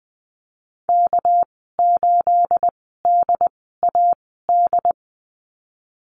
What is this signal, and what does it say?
Morse code, heard as a keyed tone, about 20 wpm, spaced as for 14 wpm, 700 Hz. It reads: K8DAD
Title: 213 Lesson 34g - Number 8 - Call Signs 20wpm